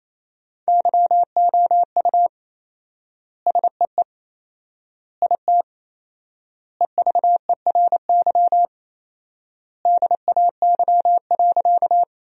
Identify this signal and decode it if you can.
Morse code, heard as a keyed tone, about 28 wpm, 700 Hz. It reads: YOU SEE IT EVERY DAY.
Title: 4x Word-Spacing - Sentences from Top 200 Words - 28wpm